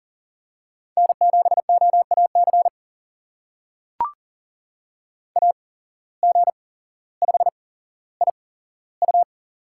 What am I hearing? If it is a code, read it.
Morse code, heard as a keyed tone, about 40 wpm, spaced as for 17 wpm, 700 Hz. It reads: N7OAC AG5IU
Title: USA Call Signs - 8x Character Spacing - 40wpm